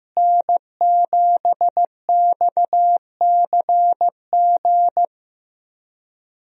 Morse code, heard as a keyed tone, about 15 wpm, 700 Hz. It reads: N7XCG
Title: USA Call Signs - 5 Characters - Repeated 3x - 15wpm